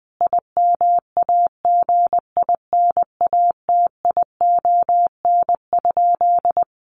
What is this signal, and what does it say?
Morse code, heard as a keyed tone, about 20 wpm, 700 Hz. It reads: IMAGINATION?
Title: Sentences from Top 1300 Words 20wpm